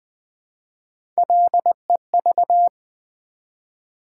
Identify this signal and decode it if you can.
Morse code, heard as a keyed tone, about 20 wpm, 700 Hz. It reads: LEV
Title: Build A Word - 1 Letter At A Time - 5-Letter Words - 20wpm